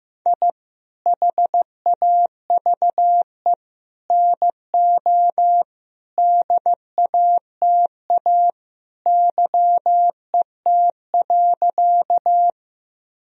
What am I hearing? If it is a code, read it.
Morse code, heard as a keyed tone, about 15 wpm, 700 Hz. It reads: I HAVE NO DATA YET.
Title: Sentences from Top 1000 Words - Review Part 3 - 15wpm